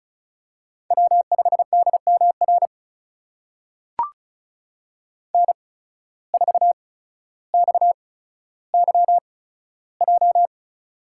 Morse code, heard as a keyed tone, about 36 wpm, spaced as for 15 wpm, 690 Hz. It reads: W5DMR N4XYJ
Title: USA Call Signs - 5 Characters - 8x Character Spacing - 35wpm